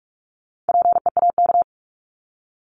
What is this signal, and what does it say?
Morse code, heard as a keyed tone, about 44 wpm, 700 Hz. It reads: PEAK
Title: Top 1751-2000 Words 45wpm